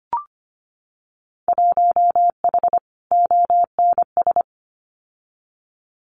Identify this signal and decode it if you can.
Morse code, heard as a keyed tone, about 25 wpm, 700 Hz. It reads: 1H ONS
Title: Winter Field Day Exchanges 25wpm